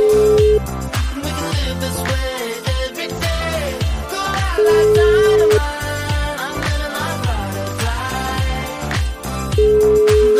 Ringback tone (Europe, 425 Hz), with noise.